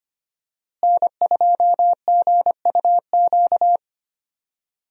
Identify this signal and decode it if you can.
Morse code, heard as a keyed tone, about 25 wpm, 700 Hz. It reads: N2GUQ
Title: USA Call Signs - 5 Characters - Repeated 2x - 25wpm